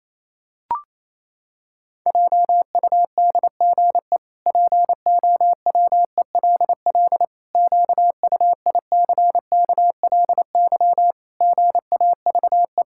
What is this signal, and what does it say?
Morse code, heard as a keyed tone, about 28 wpm, 700 Hz. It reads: JUDGE POWELL QUICKLY GAVE
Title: Pangram Challenge v2 - 28wpm